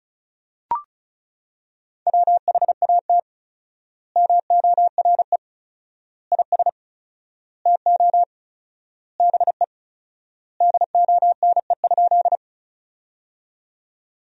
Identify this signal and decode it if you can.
Morse code, heard as a keyed tone, about 34 wpm, 700 Hz. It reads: WHAT MORE IS TO BE DONE?
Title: 4x Word-Spacing - Sentences from Top 500 Words - 35wpm